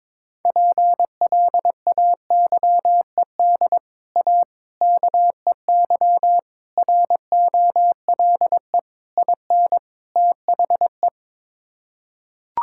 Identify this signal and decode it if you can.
Morse code, heard as a keyed tone, about 22 wpm, 700 Hz. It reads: PLAYED A KEY ROLE IN THE
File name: practice / 22wpm / Sets of 6 Words - 22wpm